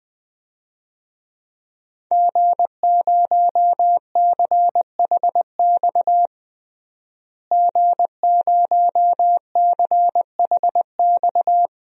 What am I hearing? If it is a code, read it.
Morse code, heard as a keyed tone, about 20 wpm, 700 Hz. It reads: G0CHX G0CHX
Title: DXCC-100 Call Signs - 5 Characters - Repeated 2x - 20wpm